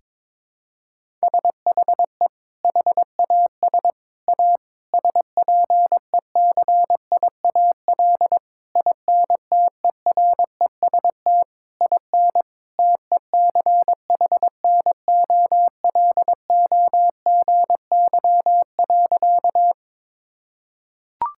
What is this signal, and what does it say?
Morse code, heard as a keyed tone, about 22 wpm, 700 Hz. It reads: SHE HAS A SPECIAL INTEREST IN TECHNOLOGY.
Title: Sentences from Top 1000 Words - Review Part 5 - 22wpm